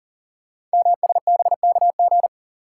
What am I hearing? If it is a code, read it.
Morse code, heard as a keyed tone, about 42 wpm, 690 Hz. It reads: MSBKG
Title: Code Groups - 5 Letters - 40wpm